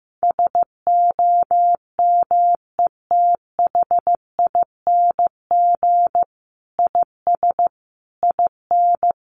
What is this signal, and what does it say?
Morse code, heard as a keyed tone, about 15 wpm, 700 Hz. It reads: SOMETHING IS IN
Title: Sentences from Top 600 Words 15wpm